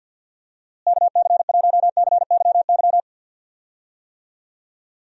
Morse code, heard as a keyed tone, about 45 wpm, 700 Hz. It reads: KC1CYY
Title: USA Call Signs - 6 Characters - Repeated 2x - 50wpm